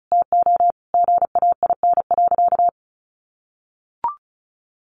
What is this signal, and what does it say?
Morse code, heard as a keyed tone, about 34 wpm, 700 Hz. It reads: TO GAIN.
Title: Sentences from Top 1100 Words 35wpm